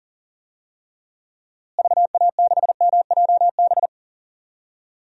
Morse code, heard as a keyed tone, about 37 wpm, 700 Hz. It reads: VA6MJB
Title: DXCC-100 Call Signs - 6 Characters - Repeated 2x - 40wpm